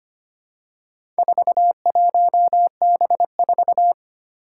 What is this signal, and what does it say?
Morse code, heard as a keyed tone, about 25 wpm, 700 Hz. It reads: 41B4
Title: Four Challenging Letter-Number Combinations 25wpm